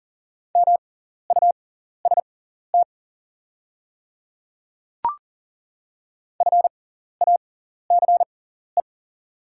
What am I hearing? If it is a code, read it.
Morse code, heard as a keyed tone, about 40 wpm, spaced as for 14 wpm, 700 Hz. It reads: MUST FACE
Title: 6x Character Spacing - Top 300 Words - 40wpm